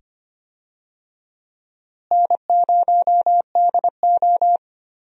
Morse code, heard as a keyed tone, about 25 wpm, 700 Hz. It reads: N0DO
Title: USA Call Signs - 4 Characters - Repeated 2x - 25wpm